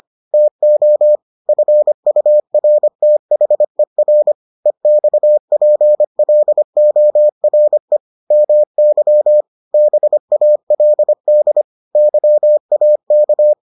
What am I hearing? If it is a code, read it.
Morse code, heard as a keyed tone, about 25 wpm, 600 Hz. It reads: TO FURTHER EXPLORE MY BALD YAK